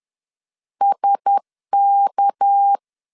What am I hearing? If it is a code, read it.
Morse code, heard as a keyed tone, about 10 wpm, 800 Hz. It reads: SK
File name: LEZIONI / 6 PKX0 / 6 PK 10wpm 2436